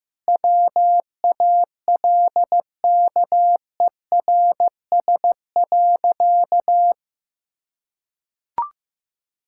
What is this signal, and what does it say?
Morse code, heard as a keyed tone, about 15 wpm, 700 Hz. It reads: WALKERS.
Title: Pangram Challenge v2 - 15wpm